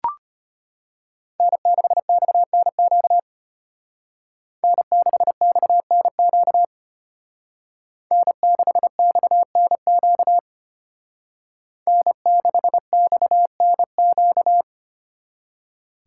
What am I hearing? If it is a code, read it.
Morse code, heard as a keyed tone, about 32 wpm, 700 Hz. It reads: N6XNQ N6XNQ N6XNQ N6XNQ